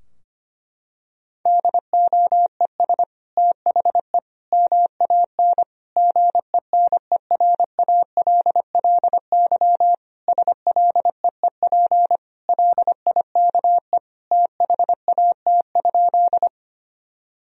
Morse code, heard as a keyed tone, about 25 wpm, 700 Hz. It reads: DOES THE MAN GENERALLY SLEEP LIKE THAT?
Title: Sentences from Top 1000 Words - Review Part 4 - 25wpm